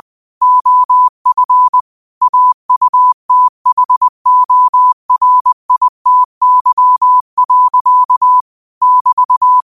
Morse code, heard as a keyed tone, about 20 wpm, 1000 Hz. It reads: OF AUTHORITY. <BT>